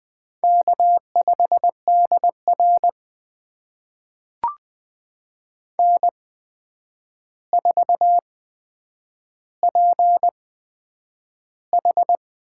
Morse code, heard as a keyed tone, about 20 wpm, spaced as for 9 wpm, 700 Hz. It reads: K5DR N4PH
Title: USA Call Signs - 4 Characters - 8x Character Spacing - 20wpm